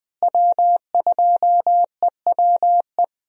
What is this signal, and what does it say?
Morse code, heard as a keyed tone, about 20 wpm, 700 Hz. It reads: W2EWE
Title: USA Call Signs - 5 Characters - Repeated 3x - 20wpm